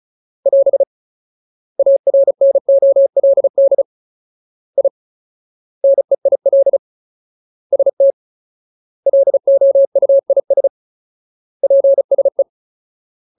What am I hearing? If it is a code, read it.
Morse code, heard as a keyed tone, about 35 wpm, 550 Hz. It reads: <AS> ARNOLD I NEIL ST LOUIS PSE
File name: Ultimate Word List / 35WPM Ultimate